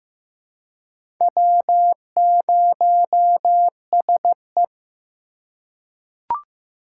Morse code, heard as a keyed tone, about 15 wpm, 700 Hz. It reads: W0SE
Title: USA Call Signs - 4 Characters - 2x Character Spacing - 15wpm